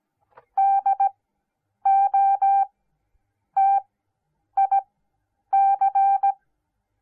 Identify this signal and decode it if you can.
Morse code, heard as a keyed tone, about 17 wpm, spaced as for 8 wpm, 790 Hz. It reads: DOTIC